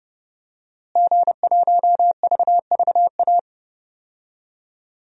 Morse code, heard as a keyed tone, about 31 wpm, 700 Hz. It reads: G1VVA
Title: DXCC-100 Call Signs - 5 Characters - Repeated 2x - 30wpm